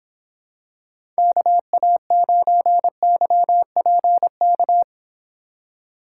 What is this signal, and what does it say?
Morse code, heard as a keyed tone, about 26 wpm, 700 Hz. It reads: KA9YPK